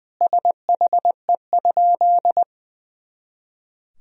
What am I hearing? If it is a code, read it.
Morse code, heard as a keyed tone, about 20 wpm, 700 Hz. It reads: SHE?
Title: Sentences from Top 200 Words and Abbreviations 20wpm